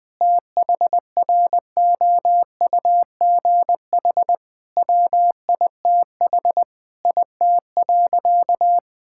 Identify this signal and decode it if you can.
Morse code, heard as a keyed tone, about 20 wpm, 700 Hz. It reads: THROUGH WITH IT.
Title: Sentences from Top 1000 Words - Review Part 6 - 20wpm